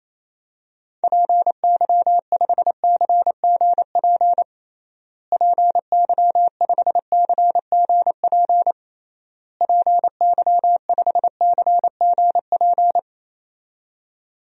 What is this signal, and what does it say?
Morse code, heard as a keyed tone, about 28 wpm, 700 Hz. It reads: PY5CGP PY5CGP PY5CGP
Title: DXCC-100 Call Signs - 6 Characters - Repeated 3x - 28wpm